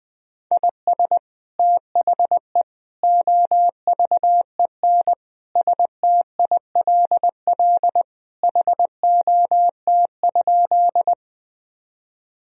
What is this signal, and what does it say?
Morse code, heard as a keyed tone, about 20 wpm, 700 Hz. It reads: IS THE OVEN STILL HOT?
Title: Sentences from Top 1000 Words - Review Part 4 - 20wpm